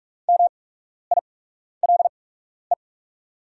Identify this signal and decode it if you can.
Morse code, heard as a keyed tone, about 44 wpm, spaced as for 12 wpm, 700 Hz. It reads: MILE
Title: 8x Character Spacing - Top 300 Words - 45wpm